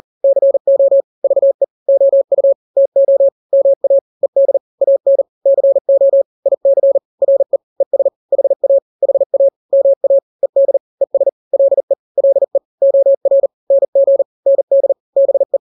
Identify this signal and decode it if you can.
Morse code, heard as a keyed tone, about 40 wpm, 550 Hz. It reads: CO VE OU TO MA ED AN CO IC RE ES HA HA MA ED ES LE LE OR NG ND BE
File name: n-grams / 2-grams 40 35 WPM